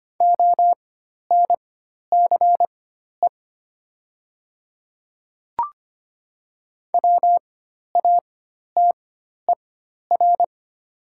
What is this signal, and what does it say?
Morse code, heard as a keyed tone, about 25 wpm, spaced as for 12 wpm, 700 Hz. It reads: ONCE WATER
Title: 4x Character Spacing - Top 300 Words - 25wpm